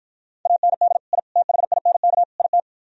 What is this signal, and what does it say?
Morse code, heard as a keyed tone, about 49 wpm, 700 Hz. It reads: AND I THINK IT